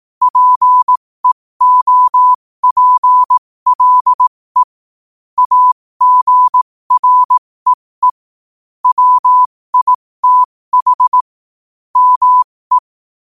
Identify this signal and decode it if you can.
Morse code, heard as a keyed tone, about 18 wpm, spaced as for 15 wpm, 1000 Hz. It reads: PEOPLE AGREE WITH ME
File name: qotdfiles / 15wpm